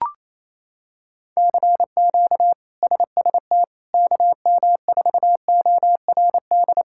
Morse code, heard as a keyed tone, about 28 wpm, 700 Hz. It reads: CQ SST KM4ORD
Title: K1USN SST QSO Practice - 28wpm